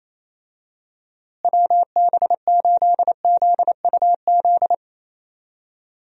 Morse code, heard as a keyed tone, about 28 wpm, 700 Hz. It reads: WB8ZUZ
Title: USA Call Signs - 6 Characters - Repeated 2x - 28wpm